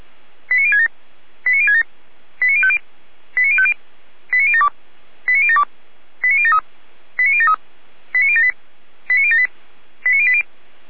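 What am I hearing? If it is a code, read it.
Five-tone selective call (ZVEI), heard as quick runs of five tones, tones of about 70 ms. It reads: 89067 89067 89050 89050 89072 89072 89073 89073 89078 89078 89080
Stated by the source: SDR-received ZVEI five-tone call 89756.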